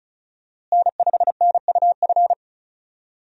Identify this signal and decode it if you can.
Morse code, heard as a keyed tone, about 36 wpm, 700 Hz. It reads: N5NUF
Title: USA Call Signs - 5 Characters - Repeated 3x - 35wpm